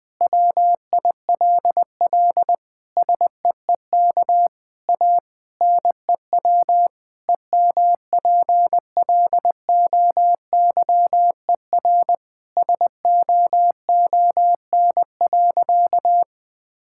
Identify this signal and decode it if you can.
Morse code, heard as a keyed tone, about 20 wpm, 700 Hz. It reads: WILL SEEK A NEW EMPLOYER SOON.
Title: Sentences from Top 1600 Words 20wpm